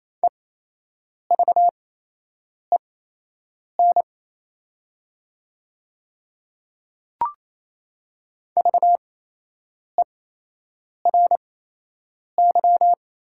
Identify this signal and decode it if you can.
Morse code, heard as a keyed tone, about 28 wpm, spaced as for 8 wpm, 700 Hz. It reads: EVEN VERY
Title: 8x Character Spacing - Top 200 Words - 28wpm